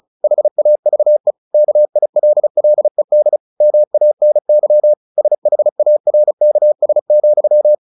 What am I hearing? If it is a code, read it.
Morse code, heard as a keyed tone, about 34 wpm, 600 Hz. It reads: HAVE KILLED MANY SHARKS,
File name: the old man and the sea 2 35wpm